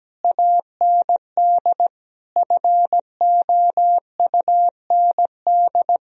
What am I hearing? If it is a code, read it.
Morse code, heard as a keyed tone, about 17 wpm, 700 Hz. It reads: AND FOUND
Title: Sentences from Top 1000 Words - Review Part 1 - 17wpm